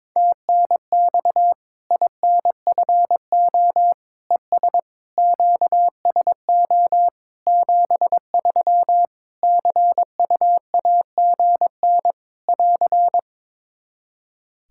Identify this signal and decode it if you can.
Morse code, heard as a keyed tone, about 22 wpm, 700 Hz. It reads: TNX INFO ES QSO 73 CUAGN <AR>
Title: Practice QSO - Exchange 4 - Ending - 22wpm